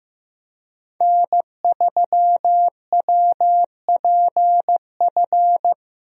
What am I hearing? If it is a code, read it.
Morse code, heard as a keyed tone, about 15 wpm, 700 Hz. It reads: N3WPF
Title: USA Call Signs - 5 Characters - Repeated 3x - 15wpm